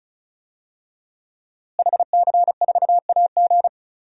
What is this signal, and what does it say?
Morse code, heard as a keyed tone, about 34 wpm, 700 Hz. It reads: HC4AG
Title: DXCC-100 Call Signs - 5 Characters - Repeated 2x - 35wpm